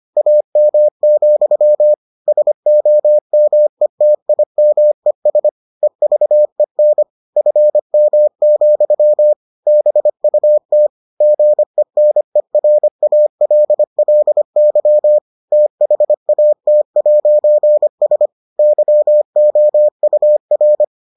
Morse code, heard as a keyed tone, about 25 wpm, 600 Hz. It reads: AM, SOMETIMES EVEN FM, BUT GENERALLY THAT'S YOUR